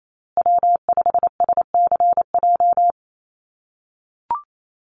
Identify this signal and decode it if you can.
Morse code, heard as a keyed tone, about 28 wpm, 700 Hz. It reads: W5SCJ
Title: USA Call Signs Repeated 2x 28wpm